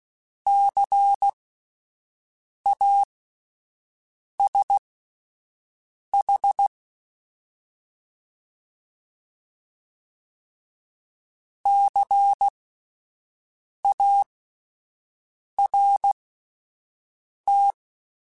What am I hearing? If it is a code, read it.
Morse code, heard as a keyed tone, about 16 wpm, spaced as for 5 wpm, 800 Hz. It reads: CASH CART
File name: Spaced Alphas / Spaced Alphas mp3s / Cs sp @10